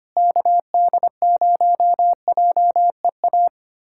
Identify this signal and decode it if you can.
Morse code, heard as a keyed tone, about 25 wpm, 700 Hz. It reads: KD0JEA
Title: USA Callsign DE Callsign - 6 Characters - 25wpm